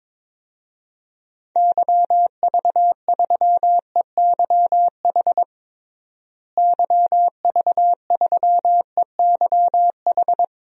Morse code, heard as a keyed tone, about 22 wpm, 700 Hz. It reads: YV3EYH YV3EYH
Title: DXCC-100 Call Signs - 6 Characters - Repeated 2x - 22wpm